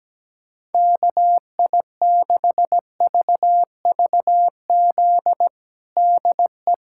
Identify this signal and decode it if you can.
Morse code, heard as a keyed tone, about 17 wpm, 700 Hz. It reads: KI6VVZ DE
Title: USA Callsign DE Callsign - 6 Characters - 17wpm